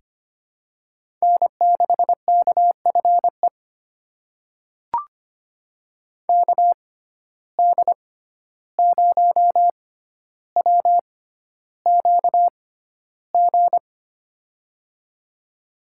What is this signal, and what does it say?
Morse code, heard as a keyed tone, about 25 wpm, spaced as for 12 wpm, 700 Hz. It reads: N6KFE KD0WQG
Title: USA Call Signs - 6x Character Spacing - 25wpm